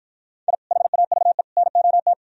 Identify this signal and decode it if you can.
Morse code, heard as a keyed tone, about 54 wpm, 700 Hz. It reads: I HAVE NOT